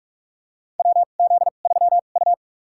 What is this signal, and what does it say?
Morse code, heard as a keyed tone, about 46 wpm, spaced as for 33 wpm, 700 Hz. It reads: WZ3U
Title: USA Call Signs - 4 Characters - 2x Character Spacing - 45wpm